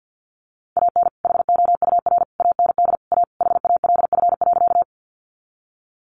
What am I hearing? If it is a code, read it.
Morse code, heard as a keyed tone, about 50 wpm, 700 Hz. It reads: AN HOUR AND A HALF.
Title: Sentences from Top 400 Words 50wpm